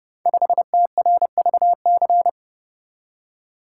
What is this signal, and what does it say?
Morse code, heard as a keyed tone, about 28 wpm, 700 Hz. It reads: 5TRVC
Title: Code Groups - 5 Letter:Numbers - 30wpm